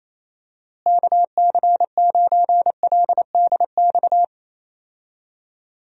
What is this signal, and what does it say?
Morse code, heard as a keyed tone, about 28 wpm, 700 Hz. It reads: KC9LDX
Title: USA Call Signs - 6 Characters - Repeated 3x - 28wpm